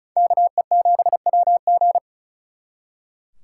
Morse code, heard as a keyed tone, about 44 wpm, 700 Hz. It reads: KE7WG
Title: Speed Racing - USA Call Signs 35wpm